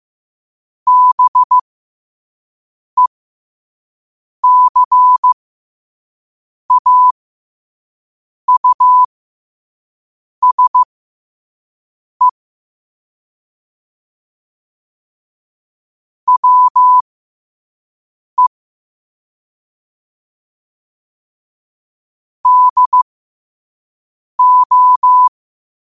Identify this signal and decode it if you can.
Morse code, heard as a keyed tone, about 15 wpm, spaced as for 5 wpm, 1000 Hz. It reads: BECAUSE WE DO
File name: qotdfiles / 5wpm